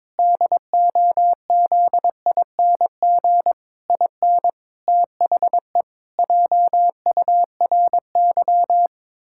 Morse code, heard as a keyed tone, about 22 wpm, 700 Hz. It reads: DOZING IN THE JURY
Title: Pangram Challenge v2 - 22wpm